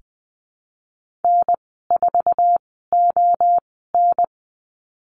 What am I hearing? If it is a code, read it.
Morse code, heard as a keyed tone, about 20 wpm, spaced as for 15 wpm, 700 Hz. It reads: N4ON
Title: 048 Lesson 9g - Number 4 - Call Signs 20wpm